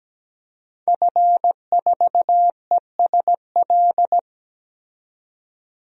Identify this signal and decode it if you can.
Morse code, heard as a keyed tone, about 17 wpm, 700 Hz. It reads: F4ESL
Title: DXCC-100 Call Signs - 5 Characters - 17wpm